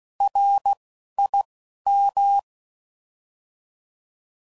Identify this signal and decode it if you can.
Morse code, heard as a keyed tone, about 16 wpm, 800 Hz. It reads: R I M
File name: Prefixes / By First Letter / Rs